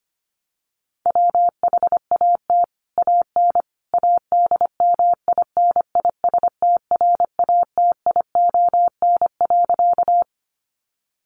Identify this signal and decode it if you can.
Morse code, heard as a keyed tone, about 25 wpm, 700 Hz. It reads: WHAT AN ADMINISTRATION.
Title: Sentences from Top 1300 Words 25wpm